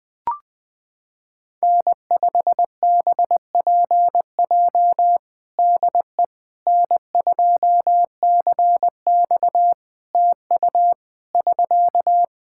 Morse code, heard as a keyed tone, about 20 wpm, 700 Hz. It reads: N5BPJ DE N2CX TU <SK>
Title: K1USN SST QSO Practice 20wpm